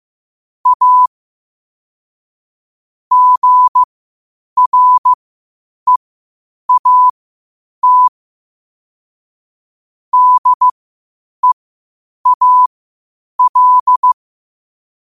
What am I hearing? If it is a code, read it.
Morse code, heard as a keyed tone, about 15 wpm, spaced as for 8 wpm, 1000 Hz. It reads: A GREAT DEAL
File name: qotdfiles / 8wpm